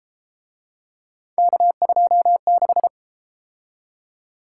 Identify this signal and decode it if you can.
Morse code, heard as a keyed tone, about 33 wpm, 700 Hz. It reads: K26